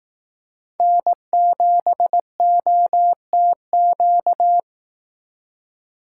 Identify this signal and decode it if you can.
Morse code, heard as a keyed tone, about 18 wpm, 700 Hz. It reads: N7OTQ
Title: Speed Racing - USA Call Signs 15wpm